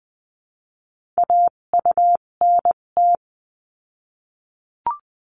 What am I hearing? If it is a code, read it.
Morse code, heard as a keyed tone, about 20 wpm, spaced as for 17 wpm, 700 Hz. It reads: AUNT
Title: 257 Lesson 42b - Words 14-20wpm